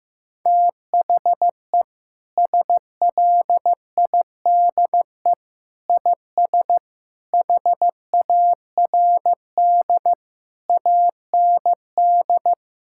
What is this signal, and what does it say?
Morse code, heard as a keyed tone, about 15 wpm, 700 Hz. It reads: THE SLIDE IS HARD AND
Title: Sentences from Top 1600 Words 15wpm